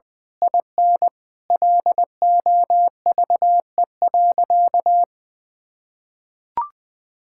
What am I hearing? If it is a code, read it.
Morse code, heard as a keyed tone, about 21 wpm, 700 Hz. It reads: IN LOVE.
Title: Sentences from Top 1600 Words 20wpm